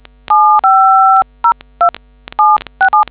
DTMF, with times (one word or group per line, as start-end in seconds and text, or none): 0.3-1.9 7502
2.4-3.0 767